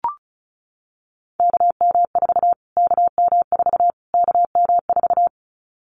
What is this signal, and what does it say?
Morse code, heard as a keyed tone, about 34 wpm, 700 Hz. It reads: KM4 KM4 KM4
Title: Callsign Prefixes for Top 25 DXCC - Repeated 3x - 35wpm